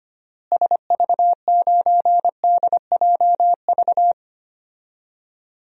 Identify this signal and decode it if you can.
Morse code, heard as a keyed tone, about 25 wpm, 700 Hz. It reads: SV9DJV
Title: DXCC-100 Call Signs - 6 Characters - Repeated 2x - 25wpm